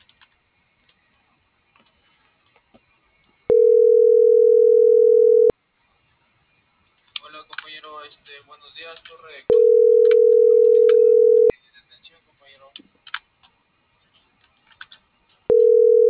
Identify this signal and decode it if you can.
Ringback tone (North America), 440 + 480 Hz.